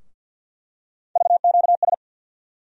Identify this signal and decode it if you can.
Morse code, heard as a keyed tone, about 52 wpm, 700 Hz. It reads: VQS
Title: USA Call Sign Postfixes - 3 Letters 50wpm